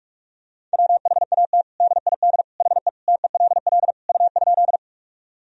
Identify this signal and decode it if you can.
Morse code, heard as a keyed tone, about 47 wpm, 700 Hz. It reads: WHAT DID HE TELL U?
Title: Sentences from Top 200 Words and Abbreviations 45wpm